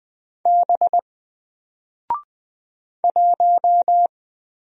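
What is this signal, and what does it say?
Morse code, heard as a keyed tone, about 20 wpm, 700 Hz. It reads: B 1
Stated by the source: Single letter and number Morse drill at 20 wpm, from N4LCD.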